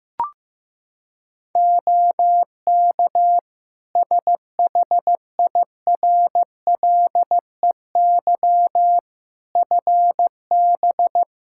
Morse code, heard as a keyed tone, about 15 wpm, 700 Hz. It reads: OK SHIRLEY FB
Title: Practice QSO - Exchange 2 - Info - 15wpm